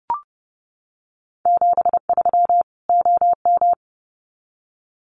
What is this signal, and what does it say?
Morse code, heard as a keyed tone, about 30 wpm, 700 Hz. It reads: 73 OM
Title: Common QSO Phrases 30wpm